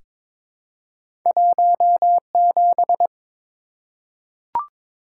Morse code, heard as a keyed tone, about 22 wpm, 700 Hz. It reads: 17
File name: practice / 22wpm / Two Letter-Number Combinations - 22wpm